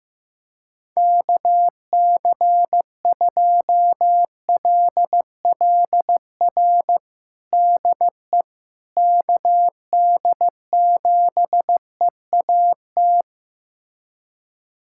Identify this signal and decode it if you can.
Morse code, heard as a keyed tone, about 15 wpm, 700 Hz. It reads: KC2LLR DE KD7EAT
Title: USA Callsign DE Callsign - 6 Characters - 15wpm